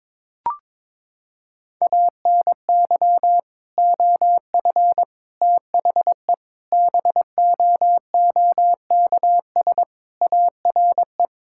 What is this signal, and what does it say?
Morse code, heard as a keyed tone, about 22 wpm, 700 Hz. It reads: ANY OF THE BOOKS ARE
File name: practice / 22wpm / Sentences with 3-Word Phrases - 22wpm